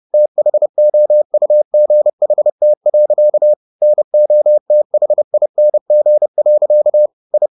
Morse code, heard as a keyed tone, about 30 wpm, 600 Hz. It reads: THOUGHT. NOTHING. I